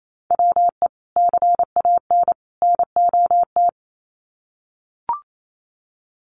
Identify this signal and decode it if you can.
Morse code, heard as a keyed tone, about 28 wpm, 700 Hz. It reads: WE CAN NOT
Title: Sets of 3 Words 28wpm